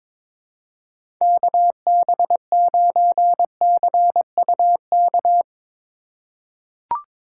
Morse code, heard as a keyed tone, about 22 wpm, 700 Hz. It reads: KB9CUK
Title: USA Call Signs - 6 Characters - 2x Character Spacing - 22wpm